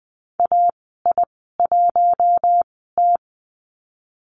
Morse code, heard as a keyed tone, about 20 wpm, spaced as for 15 wpm, 700 Hz. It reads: AI1T
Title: 042 Lesson 8g - Number 1 - Call Signs 20wpm